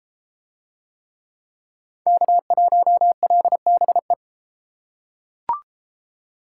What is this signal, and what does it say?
Morse code, heard as a keyed tone, about 31 wpm, 700 Hz. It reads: K1LBE